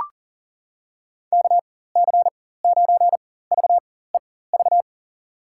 Morse code, heard as a keyed tone, about 40 wpm, spaced as for 19 wpm, 700 Hz. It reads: KC9VEV
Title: USA Call Signs - 6 Characters - 4x Character Spacing - 40wpm